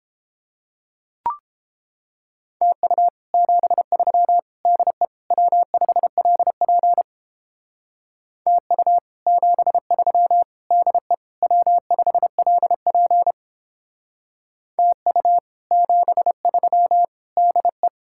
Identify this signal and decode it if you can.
Morse code, heard as a keyed tone, about 29 wpm, 700 Hz. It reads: TU 73 DE W5LP TU 73 DE W5LP TU 73 DE